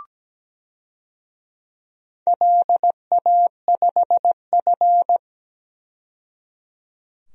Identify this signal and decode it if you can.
Morse code, heard as a keyed tone, about 17 wpm, 700 Hz. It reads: LA5F
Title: DXCC-100 Call Signs - 4 Characters - 17wpm